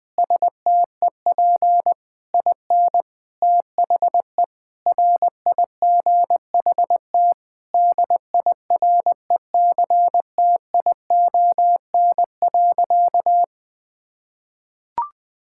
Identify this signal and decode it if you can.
Morse code, heard as a keyed tone, about 20 wpm, 700 Hz. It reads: STEP IN THE RIGHT DIRECTION.
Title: Sentences with 6-Word Phrases - Spoken Beginning - 20wpm